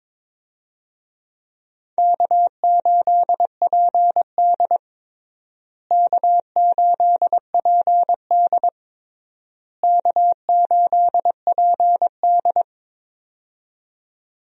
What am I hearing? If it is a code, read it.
Morse code, heard as a keyed tone, about 22 wpm, 700 Hz. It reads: K8PD K8PD K8PD
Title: USA Call Signs - 4 Characters - Repeated 3x - 22wpm